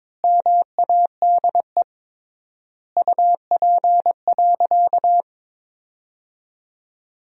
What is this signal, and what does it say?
Morse code, heard as a keyed tone, about 22 wpm, 700 Hz. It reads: MADE UP.
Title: 3x Word-Spacing - Sentences from Top 500 Words - 22wpm